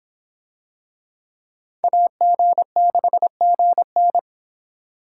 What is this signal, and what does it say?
Morse code, heard as a keyed tone, about 25 wpm, 700 Hz. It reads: AG6GN